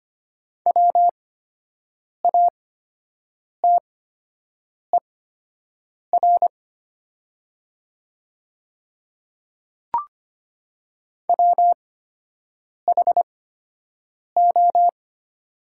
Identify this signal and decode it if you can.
Morse code, heard as a keyed tone, about 25 wpm, spaced as for 7 wpm, 700 Hz. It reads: WATER WHO
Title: 8x Character Spacing - Top 100 Words - 25wpm